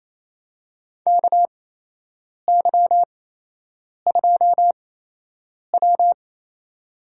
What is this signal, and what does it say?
Morse code, heard as a keyed tone, about 28 wpm, spaced as for 8 wpm, 700 Hz. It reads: KY2W
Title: USA Call Signs - 4 Characters - 8x Character Spacing - 28wpm